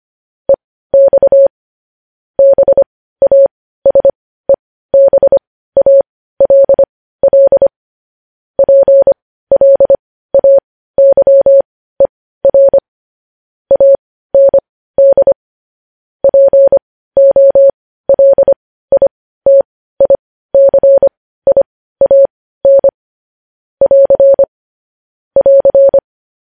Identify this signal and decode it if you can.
Morse code, heard as a keyed tone, about 25 wpm, spaced as for 15 wpm, 550 Hz. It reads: EX BASEBALL PLAYER AND POLITICIAN <AR> <AR>
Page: news-headlines-in-morse-code-at-15-wpm